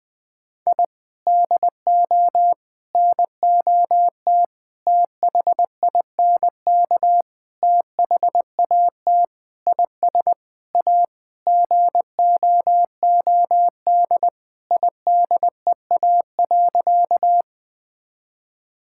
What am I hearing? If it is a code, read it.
Morse code, heard as a keyed tone, about 20 wpm, 700 Hz. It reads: I DO NOT THINK THAT IS A GOOD IDEA.
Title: Sentences with 5-Word Phrases - Spoken Beginning - 20wpm